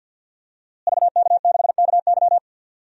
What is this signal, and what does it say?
Morse code, heard as a keyed tone, about 51 wpm, 700 Hz. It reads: VK6KY
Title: DXCC-100 Call Signs - 5 Characters - Repeated 2x - 50wpm